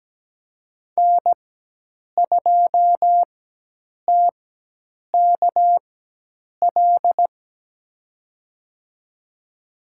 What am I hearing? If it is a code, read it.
Morse code, heard as a keyed tone, about 17 wpm, spaced as for 8 wpm, 700 Hz. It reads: N2TKL
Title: USA Call Signs - 5 Characters - 4x Character Spacing - 17wpm